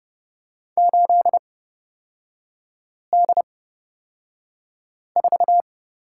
Morse code, heard as a keyed tone, about 30 wpm, spaced as for 5 wpm, 700 Hz. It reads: 8D4
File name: Lesson 36e - BT - Three Characters 30wpm